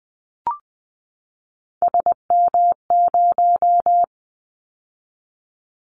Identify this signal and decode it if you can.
Morse code, heard as a keyed tone, about 20 wpm, 700 Hz. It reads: SM0
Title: Callsign Prefixes for Top 25 DXCC - 20wpm